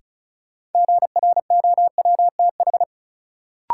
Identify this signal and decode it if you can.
Morse code, heard as a keyed tone, about 36 wpm, 700 Hz. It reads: GROWTH
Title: Sentences from Top 1000 Words - Review Part 5 - New Word Drill - 35wpm